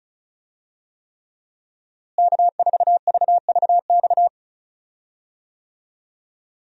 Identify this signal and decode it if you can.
Morse code, heard as a keyed tone, about 36 wpm, 700 Hz. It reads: K4VVX
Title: USA Call Signs - 5 Characters - Repeated 3x - 35wpm